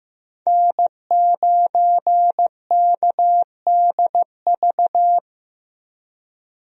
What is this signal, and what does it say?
Morse code, heard as a keyed tone, about 15 wpm, 700 Hz. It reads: N9KDV
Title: USA Call Signs - 5 Characters - Repeated 2x - 15wpm